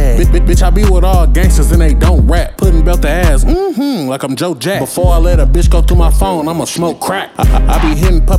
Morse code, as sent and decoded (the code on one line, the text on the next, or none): none